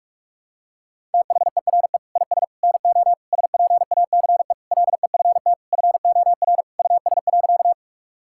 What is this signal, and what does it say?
Morse code, heard as a keyed tone, about 46 wpm, 700 Hz. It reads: THERE IS NO SPACE LEFT FOR US.
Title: Sentences from Top 1000 Words - Review Part 5 - 45wpm